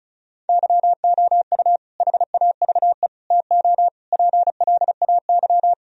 Morse code, heard as a keyed tone, about 36 wpm, 700 Hz. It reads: YOU HAVE TO PLAY
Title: Sentences from Top 1000 Words - Review Part 1 - 35wpm